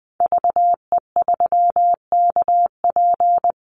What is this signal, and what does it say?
Morse code, heard as a keyed tone, about 20 wpm, 700 Hz. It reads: VE3KP
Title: Call Signs - 20wpm